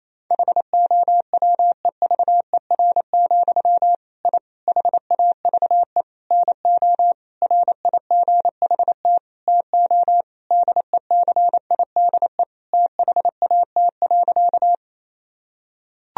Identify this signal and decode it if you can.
Morse code, heard as a keyed tone, about 28 wpm, 700 Hz. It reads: HOWEVER, I HAVE NO RIGHT TO DECIDE THAT.
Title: Sentences from Top 1000 Words - Review Part 3 - 28wpm